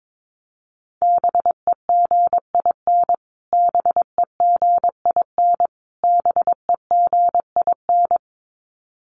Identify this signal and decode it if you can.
Morse code, heard as a keyed tone, about 22 wpm, 700 Hz. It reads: BEGIN BEGIN BEGIN
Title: Top 500 Words Repeated 22wpm